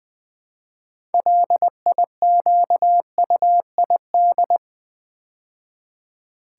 Morse code, heard as a keyed tone, about 20 wpm, 700 Hz. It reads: LIQUID
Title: Build A Word - 2 Letters At A Time - 6-Letter Words - 20wpm